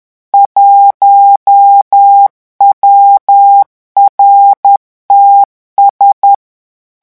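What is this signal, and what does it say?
Morse code, heard as a keyed tone, about 10 wpm, 800 Hz. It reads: 1WRTS
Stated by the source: Morse revision exercise for French radio amateurs at 10 words per minute.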